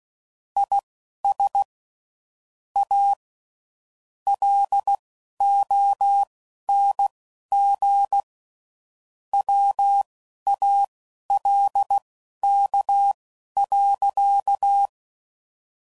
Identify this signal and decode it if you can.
Morse code, heard as a keyed tone, about 16 wpm, spaced as for 11 wpm, 800 Hz. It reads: IS A LONG WALK.